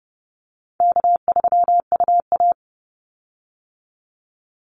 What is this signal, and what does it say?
Morse code, heard as a keyed tone, about 30 wpm, 700 Hz. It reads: K3UA
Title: Call Signs - 30wpm